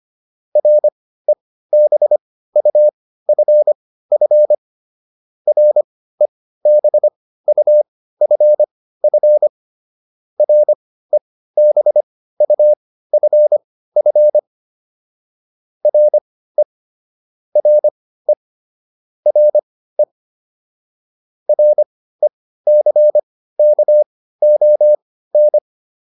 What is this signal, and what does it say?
Morse code, heard as a keyed tone, about 25 wpm, spaced as for 15 wpm, 600 Hz. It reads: REBUFF REBUFF REBUFF RE RE RE RECKON